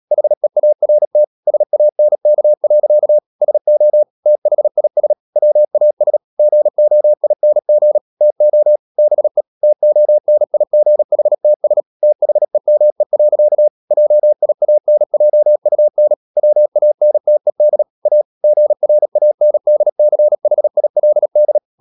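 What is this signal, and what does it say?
Morse code, heard as a keyed tone, about 37 wpm, 600 Hz. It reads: HEART SANK. SO THIS WAS GOING TO BE TONIGHTS THEME. JIANJUN WANTED A GRANDCHILD
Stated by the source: Morse code practice at 35–39 wpm sending English text of Ken Follett's novel Never, from CW med Gnister.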